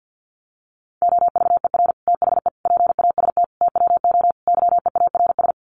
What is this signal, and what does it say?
Morse code, heard as a keyed tone, about 50 wpm, 700 Hz. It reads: OVER THE PAST TWO YEARS
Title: Sets of 5 Words Encore 50wpm